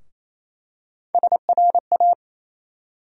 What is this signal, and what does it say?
Morse code, heard as a keyed tone, about 28 wpm, 700 Hz. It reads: SRA
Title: USA Call Sign Postfixes - 3 Letters 28wpm